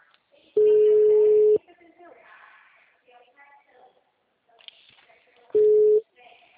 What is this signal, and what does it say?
Ringback tone (Europe, 425 Hz), with speech.